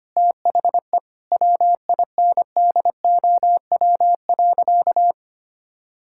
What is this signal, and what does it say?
Morse code, heard as a keyed tone, about 25 wpm, 700 Hz. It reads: THE WINDOW.
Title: Sentences from Top 1600 Words 25wpm